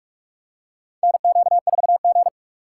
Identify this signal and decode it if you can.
Morse code, heard as a keyed tone, about 46 wpm, 700 Hz. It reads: NQ4G
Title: USA Call Signs - 2x Character Spacing - 45wpm